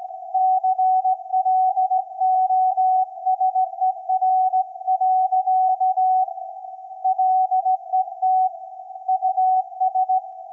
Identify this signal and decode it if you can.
Morse code, heard as a keyed tone, about 17 wpm, 750 Hz. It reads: CLOSER. LET US
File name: iz7slz / VLF / SAQ 24-12-2024